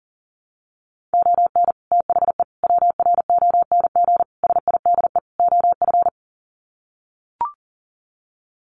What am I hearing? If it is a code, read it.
Morse code, heard as a keyed tone, about 40 wpm, 700 Hz. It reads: ON THE WRONG SIDE OF
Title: Sets of 5 Words Encore 40wpm